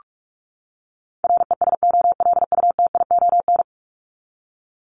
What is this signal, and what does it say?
Morse code, heard as a keyed tone, about 44 wpm, 700 Hz. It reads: RESOLUTION
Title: Top 1501-1750 Words 45wpm